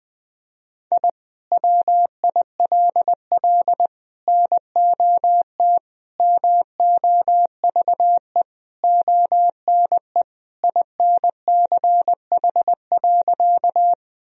Morse code, heard as a keyed tone, about 21 wpm, 700 Hz. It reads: I WILL NOT MOVE ONE INCH.
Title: Sentences from Top 1000 Words - Review Part 3 - 20wpm